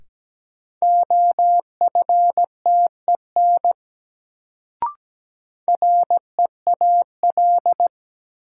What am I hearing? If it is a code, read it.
Morse code, heard as a keyed tone, about 17 wpm, 700 Hz. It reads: OFTEN REAL
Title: Sentences from Top 300 Words - New Word Drill - 17wpm